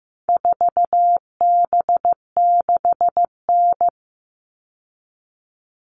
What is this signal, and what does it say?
Morse code, heard as a keyed tone, about 15 wpm, 700 Hz. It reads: 4B6N
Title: All Cut Numbers - 4-Digit 15wpm